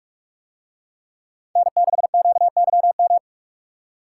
Morse code, heard as a keyed tone, about 43 wpm, 700 Hz. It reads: N6QYM